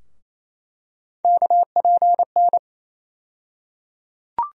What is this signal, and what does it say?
Morse code, heard as a keyed tone, about 28 wpm, 700 Hz. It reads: KPN
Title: USA Call Sign Postfixes - 3 Letters 28wpm